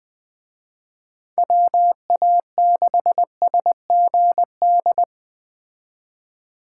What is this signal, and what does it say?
Morse code, heard as a keyed tone, about 20 wpm, 700 Hz. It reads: WA6SGD